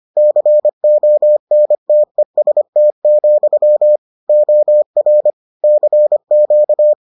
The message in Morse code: CONTEST, OR CQ

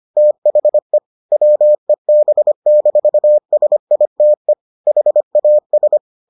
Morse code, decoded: THE WEB-SITE HAS

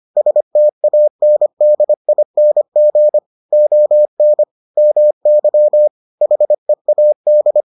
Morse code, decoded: STANDING ON MY HEAD

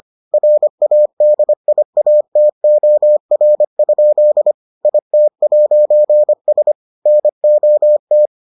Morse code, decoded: RADIATOR? IT'S NOT

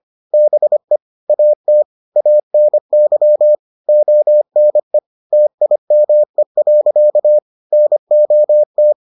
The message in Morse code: BE AT ANY ONE TIME. NOT

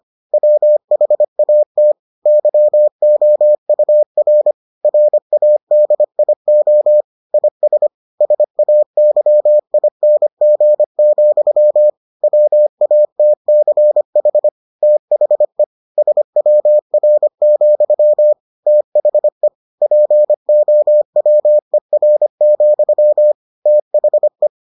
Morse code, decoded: WHAT YOUR RADIO IS SAYING, WATCH THE SWR, THE POWER, THE

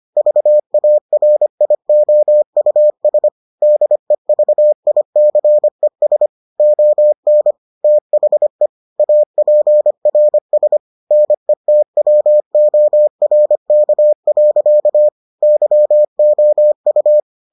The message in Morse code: VARIOUS DEVICES ON THE APRS NETWORK. YOU

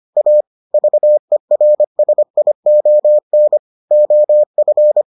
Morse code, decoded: A VERSION OF

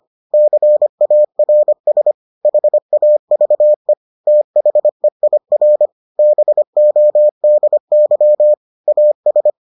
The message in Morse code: CARS HAVE THEIR BODY AS

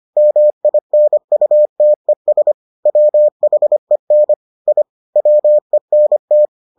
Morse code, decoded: MINUTES WHEN I WENT

